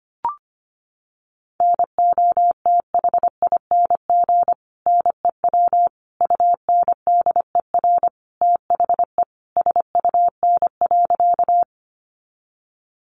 Morse code, heard as a keyed tone, about 25 wpm, 700 Hz. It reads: NOTHING NEW UNDER THE SUN.
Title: Sentences from Top 500 Words 25wpm